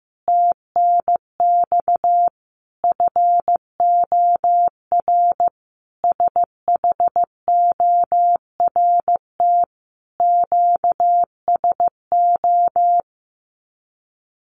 Morse code, heard as a keyed tone, about 15 wpm, 700 Hz. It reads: TNX FOR SHORT QSO
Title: Common QSO Phrases 15wpm